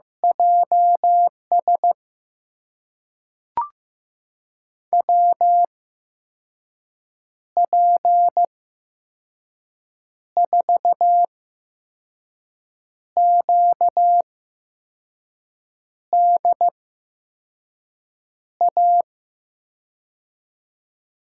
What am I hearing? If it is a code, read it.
Morse code, heard as a keyed tone, about 15 wpm, spaced as for 5 wpm, 700 Hz. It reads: JS WP4QDA
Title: USA Call Signs - 6 Characters - 8x Character Spacing - 15wpm